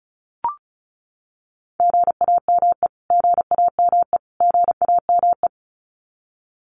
Morse code, heard as a keyed tone, about 34 wpm, 700 Hz. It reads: GAME GAME GAME
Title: Top 500 Words Repeated 35wpm